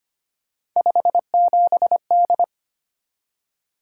Morse code, heard as a keyed tone, about 25 wpm, 700 Hz. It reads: 57D
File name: Three Letter-Number Combinations - 25wpm